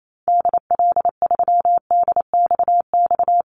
Morse code, heard as a keyed tone, about 28 wpm, 700 Hz. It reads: DL3DXX
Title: Call Signs - 28wpm